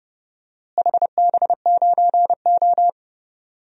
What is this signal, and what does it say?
Morse code, heard as a keyed tone, about 32 wpm, 700 Hz. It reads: HB9O
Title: DXCC-100 Call Signs - 4 Characters - Repeated 3x - 30wpm